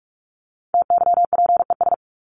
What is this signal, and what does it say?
Morse code, heard as a keyed tone, about 44 wpm, 700 Hz. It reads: TYPES
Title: ES:S Suffix - Top 100 Words with Base Words 45wpm